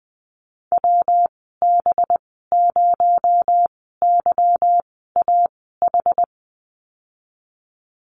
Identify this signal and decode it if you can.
Morse code, heard as a keyed tone, about 20 wpm, spaced as for 14 wpm, 700 Hz. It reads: WB0YAH
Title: 245 Lesson 39g - Letter z - Call Signs 20wpm